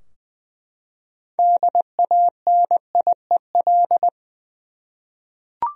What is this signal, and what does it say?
Morse code, heard as a keyed tone, about 20 wpm, 700 Hz. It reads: DANIEL
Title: USA Common Names - 20wpm